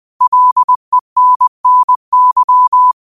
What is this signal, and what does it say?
Morse code, heard as a keyed tone, about 20 wpm, 1000 Hz. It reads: LENNY
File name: qotdfiles / 20wpm